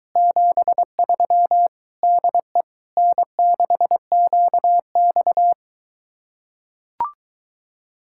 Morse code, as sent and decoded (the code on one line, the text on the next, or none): --... ...-- / -.. / -. -.... --.- -..-
73 DE N6QX